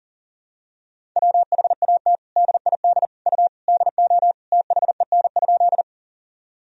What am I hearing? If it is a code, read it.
Morse code, heard as a keyed tone, about 40 wpm, 700 Hz. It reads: WHAT DID U DO THEN?
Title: Sentences from Top 100 Words and Abbreviations 40wpm